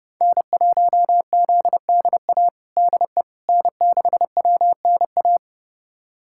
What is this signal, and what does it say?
Morse code, heard as a keyed tone, about 32 wpm, 700 Hz. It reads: N1ZDA DE N6WNA
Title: USA Callsign DE Callsign - 5 Characters - 30wpm